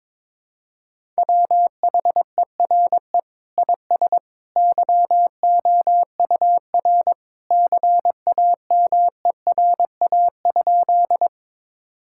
Morse code, heard as a keyed tone, about 21 wpm, 700 Hz. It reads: WHERE IS YOUR CAMERA?
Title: Sentences from Top 1000 Words - Review Part 5 - 22wpm